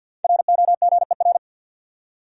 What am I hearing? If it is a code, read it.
Morse code, heard as a keyed tone, about 48 wpm, 700 Hz. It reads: ROGER
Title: USA Common Names - 50wpm